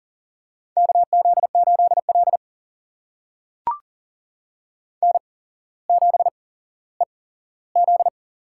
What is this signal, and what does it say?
Morse code, heard as a keyed tone, about 38 wpm, spaced as for 17 wpm, 690 Hz. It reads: KZ8L N7EZ
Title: USA Call Signs - 4 Characters - 8x Character Spacing - 40wpm